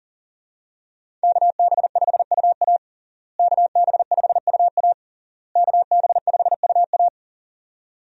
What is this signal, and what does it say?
Morse code, heard as a keyed tone, about 37 wpm, 700 Hz. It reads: KB5UA KB5UA KB5UA
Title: USA Call Signs - 5 Characters - Repeated 3x - 40wpm